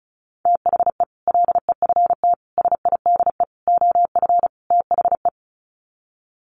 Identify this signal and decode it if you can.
Morse code, heard as a keyed tone, about 34 wpm, 700 Hz. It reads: THE LEFT SIDE OF THE
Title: Sets of 5 Words Encore 35wpm